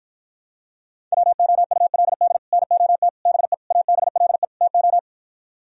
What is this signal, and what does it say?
Morse code, heard as a keyed tone, about 53 wpm, 700 Hz. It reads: WOULD NOT BE ABLE TO